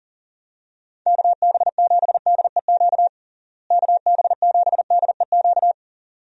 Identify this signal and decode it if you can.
Morse code, heard as a keyed tone, about 40 wpm, 700 Hz. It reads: KB7DEQ KB7DEQ
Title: USA Call Signs - 6 Characters - Repeated 2x - 40wpm